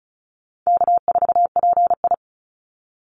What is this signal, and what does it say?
Morse code, heard as a keyed tone, about 34 wpm, 700 Hz. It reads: K4PI
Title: Easy Call Signs Repeated 2x - 35wpm